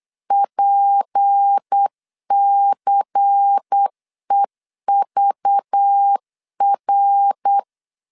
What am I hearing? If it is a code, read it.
Morse code, heard as a keyed tone, about 8 wpm, 800 Hz. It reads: PCEVR